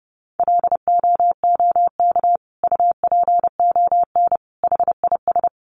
Morse code, heard as a keyed tone, about 30 wpm, 700 Hz. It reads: LOOK UPON HIS
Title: Sentences from Top 1100 Words 30wpm